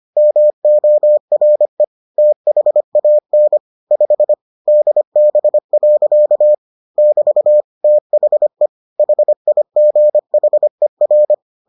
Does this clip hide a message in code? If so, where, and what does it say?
Yes: Morse code from 0.2 to 11.3 s, heard as a keyed tone, about 25 wpm, 600 Hz. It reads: MORE THAN 5 DB. <BT> THE HIGHER